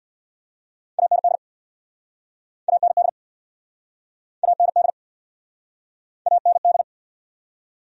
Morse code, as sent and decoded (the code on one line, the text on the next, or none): .- -. -.. / .- -. -.. / .- -. -.. / .- -. -..
AND AND AND AND